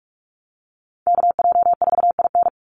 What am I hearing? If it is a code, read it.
Morse code, heard as a keyed tone, about 44 wpm, 700 Hz. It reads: KJ4IN
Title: USA Call Signs 45wpm